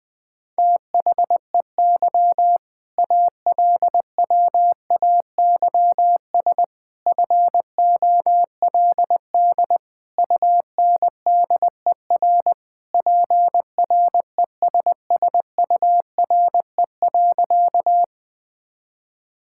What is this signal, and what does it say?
Morse code, heard as a keyed tone, about 21 wpm, 700 Hz. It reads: THEY ALWAYS FOLD UNDER PRESSURE.
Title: All-Age Appropriate Jokes - 20wpm